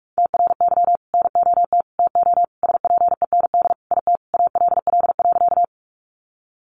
Morse code, heard as a keyed tone, about 44 wpm, 700 Hz. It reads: TRY NOT TO SPEND IT ALL.
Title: Sentences from Top 800 Words 45wpm